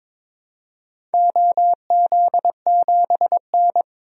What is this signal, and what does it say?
Morse code, heard as a keyed tone, about 22 wpm, 700 Hz. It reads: OZ7N
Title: DXCC-100 Call Signs - 4 Characters - 22wpm